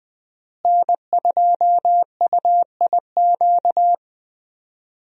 Morse code, heard as a keyed tone, about 20 wpm, 700 Hz. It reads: N2UIQ
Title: USA Call Signs - 5 Characters - Repeated 2x - 20wpm